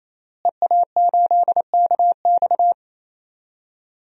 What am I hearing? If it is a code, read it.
Morse code, heard as a keyed tone, about 27 wpm, 700 Hz. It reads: EA8KX
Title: DXCC-100 Call Signs - 5 Characters - Repeated 3x - 28wpm